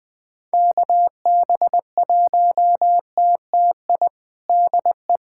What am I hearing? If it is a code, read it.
Morse code, heard as a keyed tone, about 20 wpm, 700 Hz. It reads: KB1TTI DE
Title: USA Callsign DE Callsign 20wpm